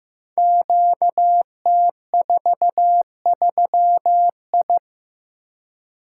Morse code, heard as a keyed tone, about 15 wpm, 700 Hz. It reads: QT43I
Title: Code Groups - 5 Letter:Numbers - 15wpm